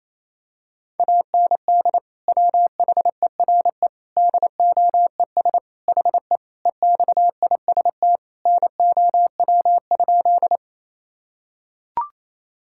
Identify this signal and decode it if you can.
Morse code, heard as a keyed tone, about 28 wpm, 700 Hz. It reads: AND WHERE DOES HE EXIST NOW?
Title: Sentences from Top 1000 Words - Review Part 5 - 28wpm